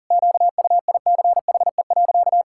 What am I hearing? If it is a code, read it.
Morse code, heard as a keyed tone, about 40 wpm, 700 Hz. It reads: QUICHE.
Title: Pangram Challenge v2 - 40wpm